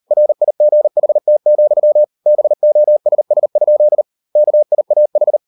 Morse code, heard as a keyed tone, about 38 wpm, 600 Hz. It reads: RIGHT, BOSS? KIAH